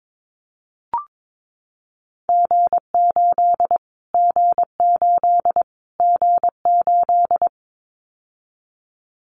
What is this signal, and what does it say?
Morse code, heard as a keyed tone, about 22 wpm, 700 Hz. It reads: G8 G8 G8
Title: Callsign Prefixes for Top 25 DXCC - Repeated 3x - 22wpm